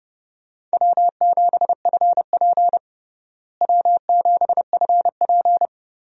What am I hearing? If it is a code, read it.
Morse code, heard as a keyed tone, about 30 wpm, 700 Hz. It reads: W7FP W7FP
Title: USA Call Signs - 4 Characters - Repeated 3x - 30wpm